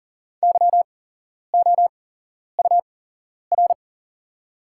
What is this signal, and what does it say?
Morse code, heard as a keyed tone, about 40 wpm, spaced as for 11 wpm, 700 Hz. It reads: YOUR
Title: 8x Character Spacing - Top 100 Words - 40wpm